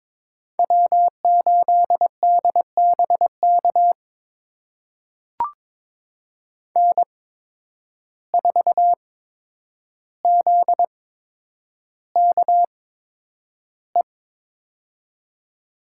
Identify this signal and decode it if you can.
Morse code, heard as a keyed tone, about 23 wpm, spaced as for 10 wpm, 700 Hz. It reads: W8DBK N4ZKE
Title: USA Call Signs - 5 Characters - 8x Character Spacing - 22wpm